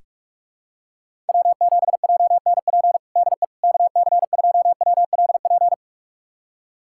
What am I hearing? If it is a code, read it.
Morse code, heard as a keyed tone, about 46 wpm, 700 Hz. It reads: W7JNP DE KC2WLP
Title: USA Callsign DE Callsign 45wpm